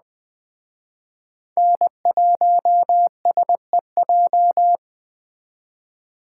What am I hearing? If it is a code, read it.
Morse code, heard as a keyed tone, about 20 wpm, 700 Hz. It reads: N1SEJ
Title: Speed Racing - USA 1x3 Call Signs 17wpm